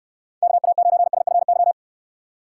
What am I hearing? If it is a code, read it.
Morse code, heard as a keyed tone, about 69 wpm, 700 Hz. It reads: KA9SFY